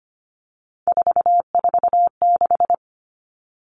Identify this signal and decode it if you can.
Morse code, heard as a keyed tone, about 25 wpm, 700 Hz. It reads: 446
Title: Numbers - 3-Digit 25wpm